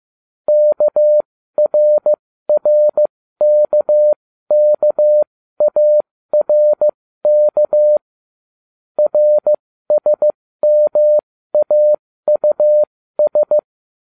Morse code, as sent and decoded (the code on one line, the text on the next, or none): -.- .-. .-. -.- -.- .- .-. -.- / .-. ... -- .- ..- ...
KRRKKARK RSMAUS